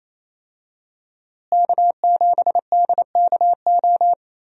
Morse code, heard as a keyed tone, about 28 wpm, 700 Hz. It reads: K7DKO
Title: USA Call Signs - 5 Characters - Repeated 2x - 28wpm